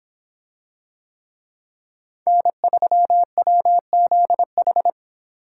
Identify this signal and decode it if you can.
Morse code, heard as a keyed tone, about 26 wpm, 700 Hz. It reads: N3WZH